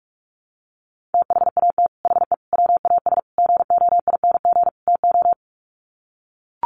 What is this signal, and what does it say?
Morse code, heard as a keyed tone, about 44 wpm, 700 Hz. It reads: THAT HE WAS GOING TO
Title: Sets of 5 Words 45wpm